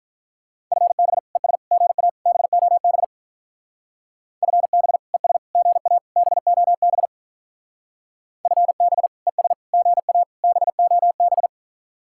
Morse code, heard as a keyed tone, about 42 wpm, 700 Hz. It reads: FB ES GA BOB FB ES GA BOB FB ES GA BOB
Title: Speed-Racing Common QSO Phrases - 35wpm